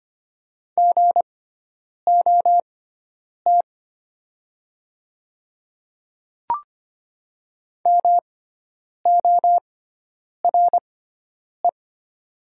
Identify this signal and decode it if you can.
Morse code, heard as a keyed tone, about 25 wpm, spaced as for 9 wpm, 700 Hz. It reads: GOT MORE
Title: 6x Character Spacing - Top 500 Words - 25wpm